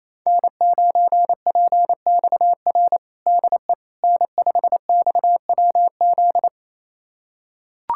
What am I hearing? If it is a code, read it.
Morse code, heard as a keyed tone, about 28 wpm, 700 Hz. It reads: N9PXR DE N5XWZ
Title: USA Callsign DE Callsign - 5 Characters - 28wpm